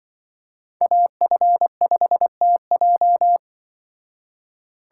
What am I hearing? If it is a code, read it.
Morse code, heard as a keyed tone, about 24 wpm, 700 Hz. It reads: AF5TJ
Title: Speed Racing - USA 2x2 Call Signs 20wpm